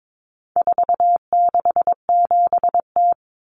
Morse code, heard as a keyed tone, about 22 wpm, 700 Hz. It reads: 467T
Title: Common Cut Numbers - 4-Digit 22wpm